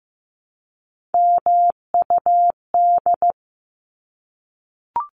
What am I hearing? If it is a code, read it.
Morse code, heard as a keyed tone, about 15 wpm, 700 Hz. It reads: MUD
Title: Top 1751-2000 Words 15wpm